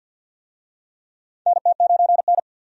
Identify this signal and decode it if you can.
Morse code, heard as a keyed tone, about 52 wpm, 700 Hz. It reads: NT9N
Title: USA Call Signs - 4 Characters - 50wpm